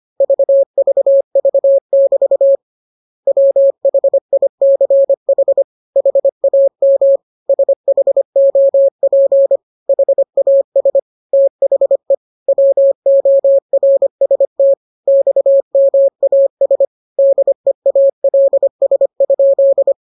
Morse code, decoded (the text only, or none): VVV<BT> WHICH HAM SHOP HAS THE WORST XMAS DEALS?